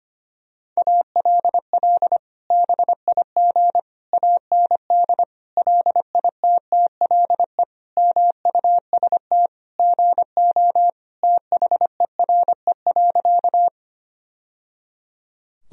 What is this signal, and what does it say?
Morse code, heard as a keyed tone, about 25 wpm, 700 Hz. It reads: ALL BIG AND LITTLE MUST GO THERE.
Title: Sentences from Top 1000 Words - Review Part 1 - 25wpm